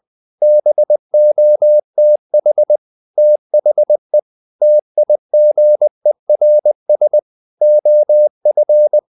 Morse code, heard as a keyed tone, about 20 wpm, 600 Hz. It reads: BOTH THE TIGERS OF
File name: the old man and the sea / the old man and the sea part 1 20wpm